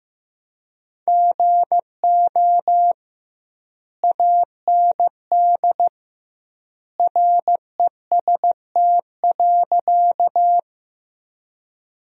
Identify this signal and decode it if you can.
Morse code, heard as a keyed tone, about 15 wpm, 700 Hz. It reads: GO AND REST.
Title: 2x Word-Spacing - Sentences from Top 500 Words - 15wpm